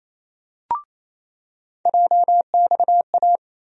Morse code, heard as a keyed tone, about 28 wpm, 700 Hz. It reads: JXA
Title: USA Call Sign Postfixes - 3 Letters 28wpm